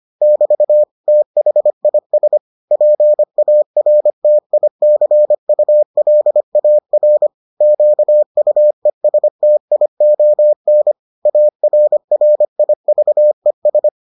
Morse code, heard as a keyed tone, about 25 wpm, 600 Hz. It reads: <BT> THIS PARTICULAR QUESTION ARRIVES